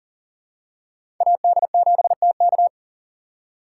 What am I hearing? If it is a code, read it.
Morse code, heard as a keyed tone, about 38 wpm, 700 Hz. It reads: AD7TK
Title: USA Call Signs - 5 Characters - 3x Character Spacing - 40wpm